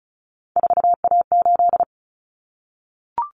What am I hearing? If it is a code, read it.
Morse code, heard as a keyed tone, about 34 wpm, 700 Hz. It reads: 4A8
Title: Common Cut Numbers - 3-Digit 35wpm